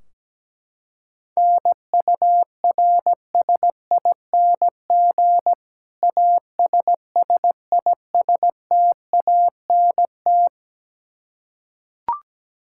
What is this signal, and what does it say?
Morse code, heard as a keyed tone, about 17 wpm, 700 Hz. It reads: NURSING ASSISTANT
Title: Practice QSO - Exchange 3 - More Info - 17wpm